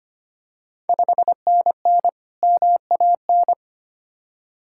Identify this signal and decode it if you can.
Morse code, heard as a keyed tone, about 25 wpm, 700 Hz. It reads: 5NN MAN
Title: Ontario QSO Party Exchange 25wpm